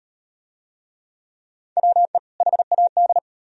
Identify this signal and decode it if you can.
Morse code, heard as a keyed tone, about 38 wpm, 700 Hz. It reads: WE HAD